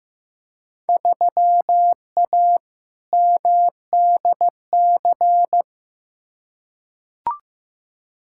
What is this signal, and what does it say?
Morse code, heard as a keyed tone, about 15 wpm, 700 Hz. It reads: 3A MDC
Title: Field Day Exchanges - 15wpm